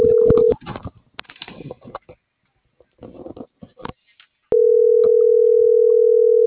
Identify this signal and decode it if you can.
Ringback tone (North America), 440 + 480 Hz.